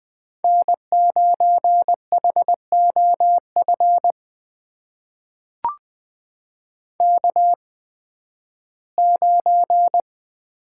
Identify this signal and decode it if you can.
Morse code, heard as a keyed tone, about 20 wpm, spaced as for 13 wpm, 700 Hz. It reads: N9HOF K9
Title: USA Call Signs - 5 Characters - 8x Character Spacing - 20wpm